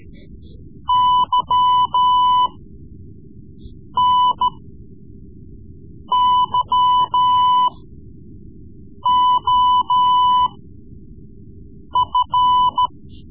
Morse code, heard as a keyed tone, about 14 wpm, spaced as for 5 wpm, 1000 Hz. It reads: YNYOF
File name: Morse Lessons / Morse Lessons / Tape-3B